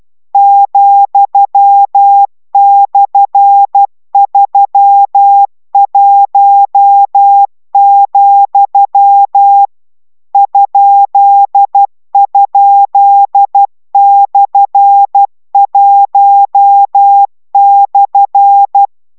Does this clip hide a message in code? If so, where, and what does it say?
Yes: Morse code from 0.3 to 18.9 s, heard as a keyed tone, about 12 wpm, 800 Hz. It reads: ,/31, ??/1/